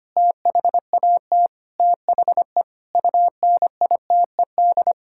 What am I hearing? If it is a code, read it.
Morse code, heard as a keyed tone, about 25 wpm, 700 Hz. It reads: THAT THE UNITED
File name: practice / 25wpm / Sentences with 4-Word Phrases - 25wpm